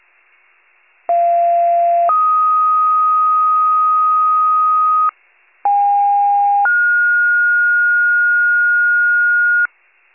Two-tone page, heard as a one-second tone, then a three-second tone, about 690 then 1250 Hz, about 810 then 1430 Hz. It is heard from 1.1 to 9.7 s.